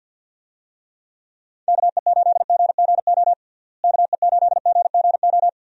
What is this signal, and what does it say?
Morse code, heard as a keyed tone, about 52 wpm, 700 Hz. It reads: KE8GGO KE8GGO
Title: USA Call Signs - 6 Characters - Repeated 2x - 50wpm